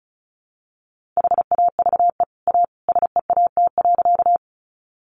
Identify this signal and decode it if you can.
Morse code, heard as a keyed tone, about 34 wpm, 700 Hz. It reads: HAVE A SEAT.
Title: Sentences from Top 1500 Words 35wpm